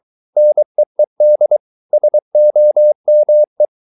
Morse code, heard as a keyed tone, about 23 wpm, 600 Hz. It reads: NEED SOME